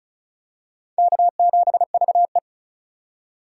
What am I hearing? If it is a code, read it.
Morse code, heard as a keyed tone, about 36 wpm, 700 Hz. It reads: K7VE
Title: USA Call Signs - 4 Characters - 3x Character Spacing - 35wpm